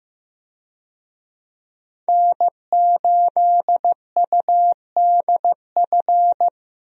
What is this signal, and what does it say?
Morse code, heard as a keyed tone, about 15 wpm, 700 Hz. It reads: N8UDF
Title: USA Call Signs - 5 Characters - Repeated 2x - 15wpm